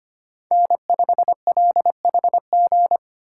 Morse code, heard as a keyed tone, about 25 wpm, 700 Hz. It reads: N5LHG
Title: USA Call Signs - 5 Characters - 3x Character Spacing - 25wpm